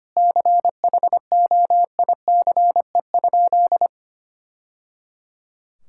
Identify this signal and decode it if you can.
Morse code, heard as a keyed tone, about 25 wpm, 700 Hz. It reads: CHOICE?
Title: Sentences from Top 1000 Words - Review Part 6 - 25wpm